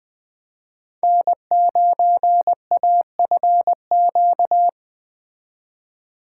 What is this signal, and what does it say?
Morse code, heard as a keyed tone, about 20 wpm, 700 Hz. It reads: N9AFQ
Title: Speed Racing - USA 1x3 Call Signs 17wpm